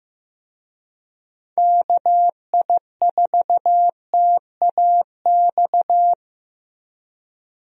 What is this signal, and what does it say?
Morse code, heard as a keyed tone, about 15 wpm, 700 Hz. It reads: KI4TAX
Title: USA Call Signs - 6 Characters - Repeated 3x - 15wpm